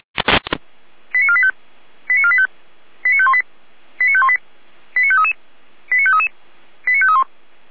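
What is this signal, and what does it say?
Five-tone selective call (ZVEI), heard as quick runs of five tones, tones of about 70 ms. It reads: 89485 89485 89518 89518 89533 89533 89531
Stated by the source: ZVEI five-tone selective call 89462 received with an SDR.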